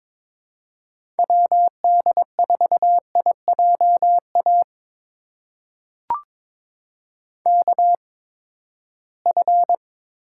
Morse code, heard as a keyed tone, about 22 wpm, spaced as for 15 wpm, 700 Hz. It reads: WD4IJA KF
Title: USA Call Signs - 6 Characters - 8x Character Spacing - 22wpm